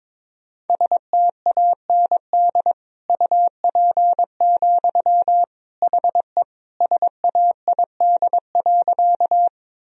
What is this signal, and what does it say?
Morse code, heard as a keyed tone, about 22 wpm, 700 Hz. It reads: STAND UP, HE SAID.
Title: Sentences from Top 1000 Words - Review Part 1 - 22wpm